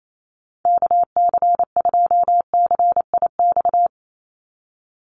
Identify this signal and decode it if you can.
Morse code, heard as a keyed tone, about 28 wpm, 700 Hz. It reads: KC2CIX
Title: USA Call Signs 28wpm